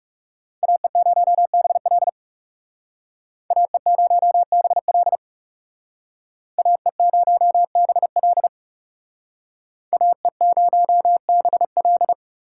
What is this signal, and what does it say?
Morse code, heard as a keyed tone, about 36 wpm, 700 Hz. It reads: AE0BL AE0BL AE0BL AE0BL
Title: Speed Racing - USA 2x2 Call Signs 30wpm